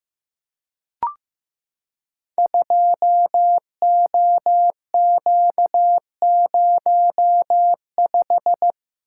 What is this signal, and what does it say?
Morse code, heard as a keyed tone, about 15 wpm, 700 Hz. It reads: 2OQ05
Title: Code Groups - 5 Letter:Numbers - 15wpm